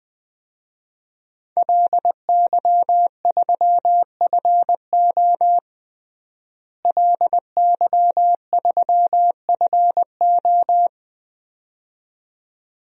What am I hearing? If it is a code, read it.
Morse code, heard as a keyed tone, about 21 wpm, 700 Hz. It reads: LY3FO LY3FO
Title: DXCC-100 Call Signs - 5 Characters - Repeated 2x - 20wpm